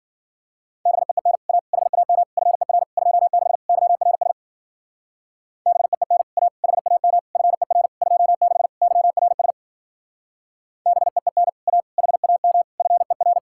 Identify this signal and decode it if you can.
Morse code, heard as a keyed tone, about 47 wpm, 700 Hz. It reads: BEEN A HAM FER 26 YRS BEEN A HAM FER 26 YRS BEEN A HAM FER
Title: Speed-Racing Common QSO Phrases Encore - 40wpm